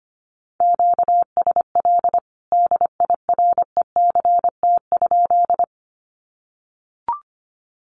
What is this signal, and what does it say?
Morse code, heard as a keyed tone, about 25 wpm, 700 Hz. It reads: QSL DIRECT?
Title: Common QSO Phrases 25wpm